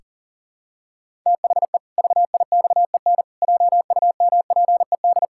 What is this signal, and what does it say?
Morse code, heard as a keyed tone, about 40 wpm, 700 Hz. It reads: THE VIXEN JUMPED
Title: Pangram Challenge v2 - 40wpm